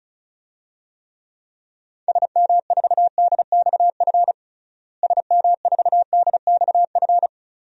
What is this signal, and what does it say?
Morse code, heard as a keyed tone, about 34 wpm, 700 Hz. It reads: SM4DXF SM4DXF
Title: DXCC-100 Call Signs - 6 Characters - Repeated 2x - 35wpm